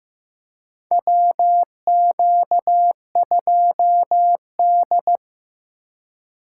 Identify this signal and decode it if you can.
Morse code, heard as a keyed tone, about 15 wpm, 700 Hz. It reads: WQ2D
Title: USA Call Signs - 4 Characters - 2x Character Spacing - 15wpm